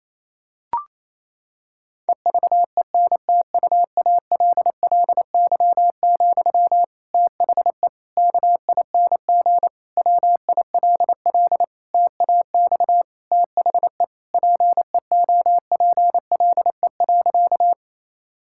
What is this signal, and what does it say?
Morse code, heard as a keyed tone, about 28 wpm, 700 Hz. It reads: EVENTUALLY, THE KING WILL TAX THE PEOPLE.
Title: Sentences from Top 1000 Words - Review Part 6 - 28wpm